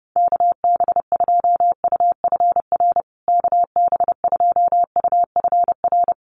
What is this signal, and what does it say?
Morse code, heard as a keyed tone, about 30 wpm, 700 Hz. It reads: KB2UFR KB2UFR
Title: USA Call Signs Repeated 3x 30wpm